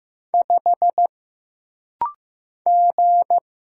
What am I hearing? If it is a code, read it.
Morse code, heard as a keyed tone, about 15 wpm, 700 Hz. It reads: 5 G
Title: Single Letter-Number Alt - Rapid-Fire 15wpm